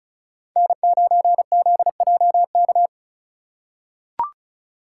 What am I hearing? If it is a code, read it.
Morse code, heard as a keyed tone, about 34 wpm, 700 Hz. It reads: N9ZJK
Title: USA Call Signs - 5 Characters - 3x Character Spacing - 35wpm